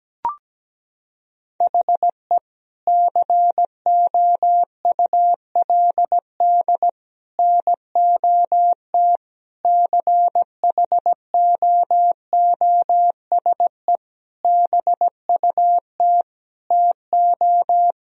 Morse code, heard as a keyed tone, about 17 wpm, 700 Hz. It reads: HE COULD NOT CHOOSE BUT TO
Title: Sentences from Top 1000 Words - Review Part 4 - 17wpm